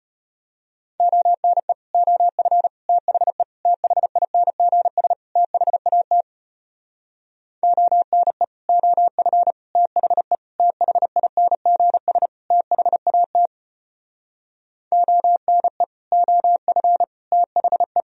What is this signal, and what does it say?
Morse code, heard as a keyed tone, about 33 wpm, 700 Hz. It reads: ONE OF THE THINGS THAT ONE OF THE THINGS THAT ONE OF THE